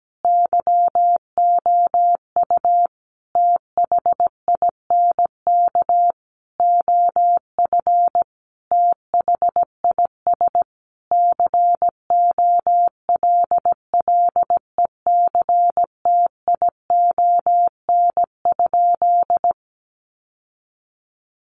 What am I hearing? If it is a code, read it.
Morse code, heard as a keyed tone, about 17 wpm, 700 Hz. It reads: YOU THINK OF THIS COLLECTION?
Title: Sentences from Top 1200 Words 17wpm